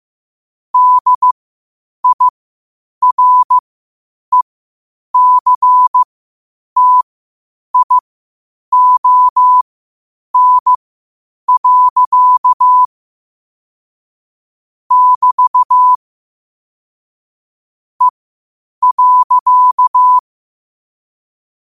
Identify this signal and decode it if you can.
Morse code, heard as a keyed tone, about 15 wpm, spaced as for 8 wpm, 1000 Hz. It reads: DIRECTION. <BT> E.
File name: qotdfiles / 8wpm